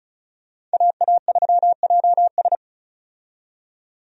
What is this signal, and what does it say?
Morse code, heard as a keyed tone, about 34 wpm, 700 Hz. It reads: AA3JS
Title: USA Call Signs - 5 Characters - 35wpm